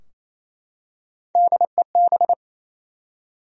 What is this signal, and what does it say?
Morse code, heard as a keyed tone, about 28 wpm, 700 Hz. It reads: DEB
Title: USA Common Names - Max 3 Letters - 28wpm